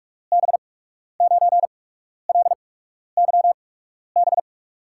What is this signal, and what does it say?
Morse code, heard as a keyed tone, about 44 wpm, spaced as for 12 wpm, 700 Hz. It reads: B9LYB
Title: USA Call Signs - 6 Characters - 8x Character Spacing - 45wpm